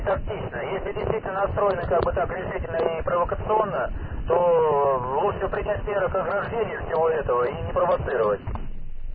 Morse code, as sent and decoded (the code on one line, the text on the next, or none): none